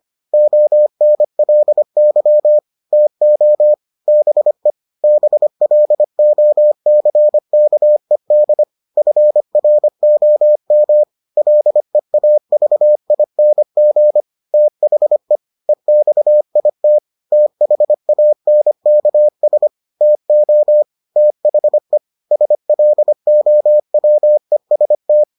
Morse code, heard as a keyed tone, about 25 wpm, 600 Hz. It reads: ONLY TO BE BLOCKED FROM LEAVING THE EXIT THANKS TO THE SLOWEST